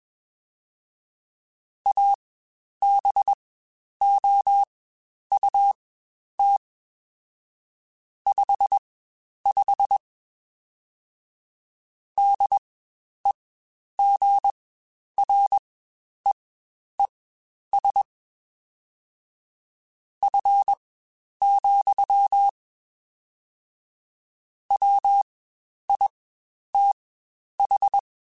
Morse code, heard as a keyed tone, about 21 wpm, spaced as for 10 wpm, 800 Hz. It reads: ABOUT 55 DEGREES F, WITH
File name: Spaced Characters / Spaced Characters-mp3s / Weather 4 24 sp @20